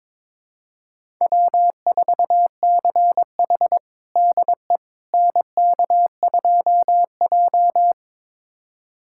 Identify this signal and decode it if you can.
Morse code, heard as a keyed tone, about 22 wpm, 700 Hz. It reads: W4CH DE NK2J
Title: USA Callsign DE Callsign - 4 Characters - 22wpm